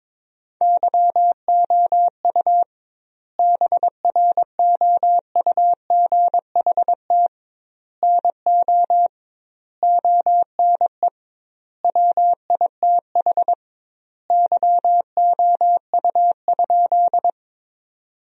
Morse code, heard as a keyed tone, about 22 wpm, 700 Hz. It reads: YOU BROUGHT NO ONE WITH YOU?
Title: 2x Word-Spacing - Sentences from Top 500 Words - 22wpm